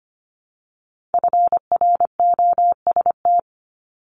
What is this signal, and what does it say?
Morse code, heard as a keyed tone, about 25 wpm, 700 Hz. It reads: FROST
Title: Multi-letter Phonemes 25wpm